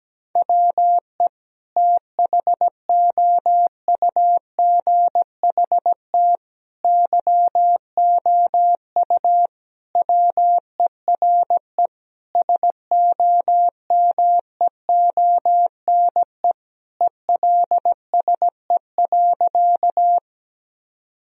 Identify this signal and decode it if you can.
Morse code, heard as a keyed tone, about 17 wpm, 700 Hz. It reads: WE THOUGHT YOU WERE SOMEONE ELSE.
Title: Sentences from Top 1000 Words - Review Part 4 - 17wpm